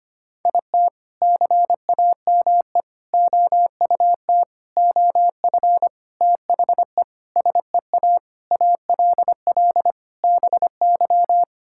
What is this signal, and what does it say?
Morse code, heard as a keyed tone, about 25 wpm, 700 Hz. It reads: IT CAME OUT OF THE SEA ALL BY
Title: Sentences from Top 1000 Words - Review Part 5 - 25wpm